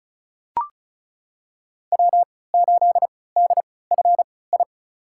Morse code, heard as a keyed tone, about 36 wpm, spaced as for 20 wpm, 690 Hz. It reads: W8DFI
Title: USA Call Signs - 5 Characters - 3x Character Spacing - 35wpm